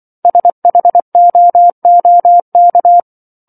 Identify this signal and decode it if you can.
Morse code, heard as a keyed tone, about 24 wpm, 700 Hz. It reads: SHOOK